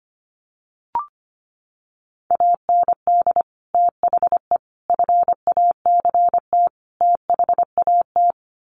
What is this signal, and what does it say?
Morse code, heard as a keyed tone, about 25 wpm, 700 Hz. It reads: AND THE FACT THAT
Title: Sets of 4 Words 25wpm